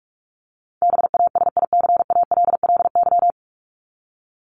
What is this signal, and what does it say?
Morse code, heard as a keyed tone, about 44 wpm, 700 Hz. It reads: BASICALLY
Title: Top 1251-1500 Words 45wpm